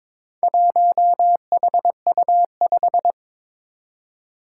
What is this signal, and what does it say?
Morse code, heard as a keyed tone, about 21 wpm, 700 Hz. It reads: 1HU5
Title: Four Challenging Letter-Number Combinations 22wpm